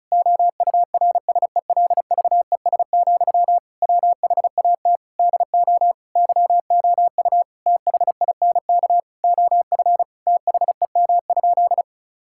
Morse code, heard as a keyed tone, about 34 wpm, 700 Hz. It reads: OURSELVES, WHAT DO YOU THINK OF THEM?
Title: Sentences from Top 1600 Words 35wpm